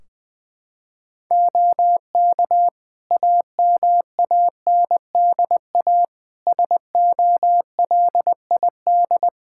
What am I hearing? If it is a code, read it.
Morse code, heard as a keyed tone, about 20 wpm, 700 Hz. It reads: OK AMANDA SOLID
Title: K1USN SST QSO Practice 20wpm